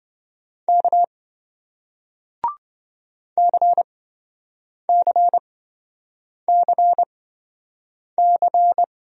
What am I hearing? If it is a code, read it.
Morse code, heard as a keyed tone, about 26 wpm, spaced as for 7 wpm, 700 Hz. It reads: K CCCC